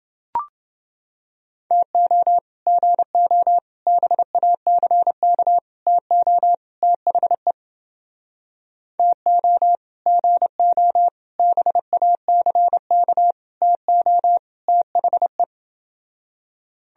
Morse code, heard as a keyed tone, about 29 wpm, 700 Hz. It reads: TO GO BACK TO THE TO GO BACK TO THE